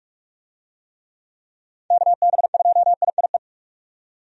Morse code, heard as a keyed tone, about 46 wpm, 700 Hz. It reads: KB2IIE
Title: USA Call Signs - 6 Characters - 45wpm